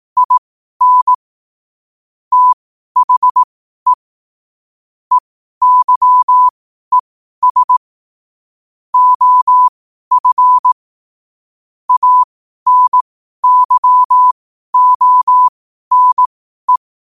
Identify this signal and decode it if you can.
Morse code, heard as a keyed tone, about 18 wpm, spaced as for 12 wpm, 1000 Hz. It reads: IN THE EYES OF ANYONE
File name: qotdfiles / 12wpm